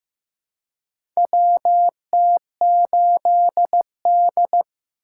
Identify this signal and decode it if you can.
Morse code, heard as a keyed tone, about 15 wpm, 700 Hz. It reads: WT8D
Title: USA Call Signs - 3x Character Spacing - 15wpm